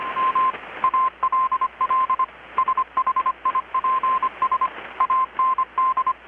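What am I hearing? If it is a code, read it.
Morse code, heard as a keyed tone, about 25 wpm, 1040 Hz, with radio noise. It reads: M ALL SHIPS AND
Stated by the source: Morse from KPH's Night of Nights XX maritime radiotelegraphy event, received in Chiapas, Mexico about 3,800 km away on a Kenwood R-600 with a grounded 10 m long-wire antenna.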